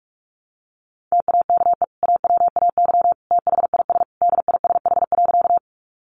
Morse code, heard as a keyed tone, about 44 wpm, 700 Hz. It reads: TAKE AWAY THIS DISH.
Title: Sentences from Top 1300 Words 45wpm